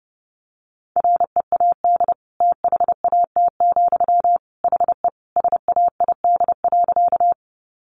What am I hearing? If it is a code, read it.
Morse code, heard as a keyed tone, about 30 wpm, 700 Hz. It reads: READ THAT, HE SAID.
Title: Sentences from Top 200 Words 30wpm